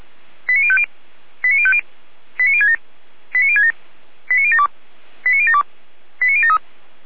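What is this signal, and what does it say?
Five-tone selective call (ZVEI), heard as quick runs of five tones, tones of about 70 ms. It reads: 89050 89050 89067 89067 89072 89072 89073
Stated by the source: Five-tone ZVEI selective call received on an SDR.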